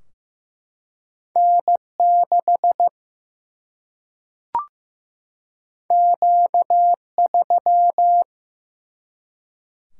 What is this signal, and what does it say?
Morse code, heard as a keyed tone, about 16 wpm, 700 Hz. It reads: N6 Q3
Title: Single-Letter Single-Number 15wpm